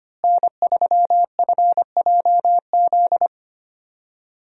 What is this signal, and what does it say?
Morse code, heard as a keyed tone, about 25 wpm, 700 Hz. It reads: N3FJZ
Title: USA Call Signs - 5 Characters - Repeated 2x - 25wpm